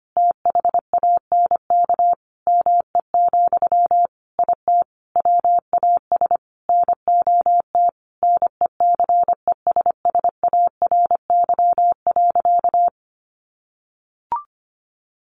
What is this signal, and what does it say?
Morse code, heard as a keyed tone, about 25 wpm, 700 Hz. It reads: THANK ME, IT WAS NOT NECESSARY.
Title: Sentences from Top 800 Words 25wpm